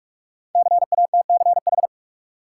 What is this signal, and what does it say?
Morse code, heard as a keyed tone, about 44 wpm, 700 Hz. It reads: CATCH
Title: Sentences from Top 1500 Words - New Word Drill Alt - 45wpm